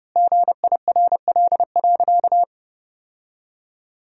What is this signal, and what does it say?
Morse code, heard as a keyed tone, about 30 wpm, 700 Hz. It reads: GIRL.
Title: 3x Word-Spacing - Sentences from Top 400 Words - 30wpm